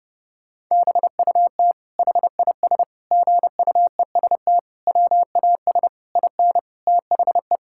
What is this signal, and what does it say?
Morse code, heard as a keyed tone, about 30 wpm, 700 Hz. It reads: BUT HIS GUEST WAS IN THE
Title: Sentences from Top 1600 Words 30wpm